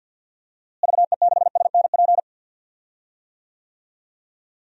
Morse code, heard as a keyed tone, about 52 wpm, 700 Hz. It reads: VE6SNP
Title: DXCC-100 Call Signs - 6 Characters - Repeated 2x - 50wpm